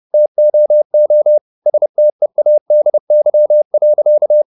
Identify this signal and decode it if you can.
Morse code, heard as a keyed tone, about 30 wpm, 600 Hz. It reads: TOO STEADY.
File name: the old man and the sea / the old man and the sea 1 30wpm